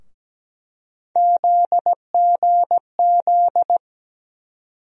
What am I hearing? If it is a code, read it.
Morse code, heard as a keyed tone, about 17 wpm, 700 Hz. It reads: ZGZ
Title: USA Call Sign Postfixes - 3 Letters 17wpm